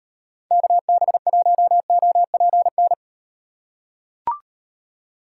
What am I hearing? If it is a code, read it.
Morse code, heard as a keyed tone, about 34 wpm, 700 Hz. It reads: KB1OPN